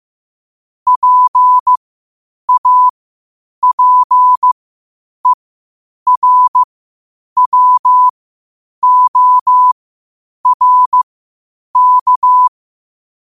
Morse code, heard as a keyed tone, about 15 wpm, spaced as for 8 wpm, 1000 Hz. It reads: PAPERWORK
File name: 8wpm